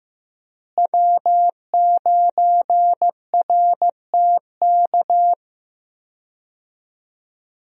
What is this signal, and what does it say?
Morse code, heard as a keyed tone, about 15 wpm, 700 Hz. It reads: W9RTK
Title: USA Call Signs - 5 Characters - Repeated 2x - 15wpm